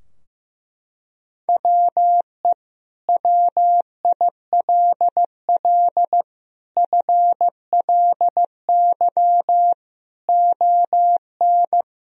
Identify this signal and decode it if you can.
Morse code, heard as a keyed tone, about 16 wpm, 700 Hz. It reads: WE WILL FLY ON
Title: Sentences from Top 1000 Words - Review Part 3 - 15wpm